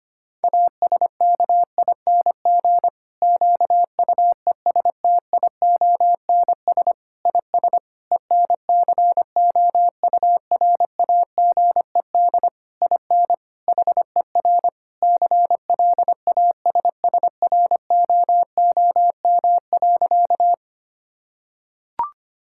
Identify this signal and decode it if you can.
Morse code, heard as a keyed tone, about 25 wpm, 700 Hz. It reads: ASKING QUESTIONS IS ENCOURAGED IN HER CLASSROOM.
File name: practice / 25wpm / Difficult Sentences with Hints - 25wpm